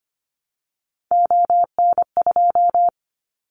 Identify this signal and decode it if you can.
Morse code, heard as a keyed tone, about 25 wpm, 700 Hz. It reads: ON2
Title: Callsign Prefixes for Top 25 DXCC - 25wpm